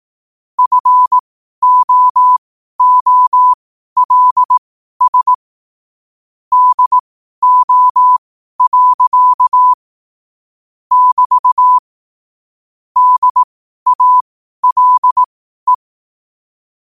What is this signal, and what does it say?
Morse code, heard as a keyed tone, about 18 wpm, spaced as for 12 wpm, 1000 Hz. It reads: FOOLS DO. <BT> DALE